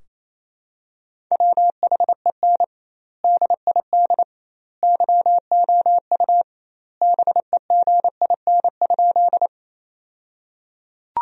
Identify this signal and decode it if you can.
Morse code, heard as a keyed tone, about 28 wpm, 700 Hz. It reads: WHEN DID YOU BEGIN?
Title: 2x Word-Spacing - Sentences from Top 300 Words - 28wpm